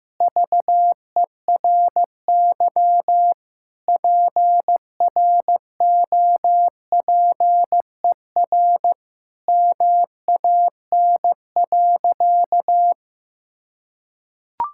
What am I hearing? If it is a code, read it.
Morse code, heard as a keyed tone, about 15 wpm, 700 Hz. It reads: VERY PROPER MAN.
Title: Sentences from Top 1000 Words - Review Part 5 - 15wpm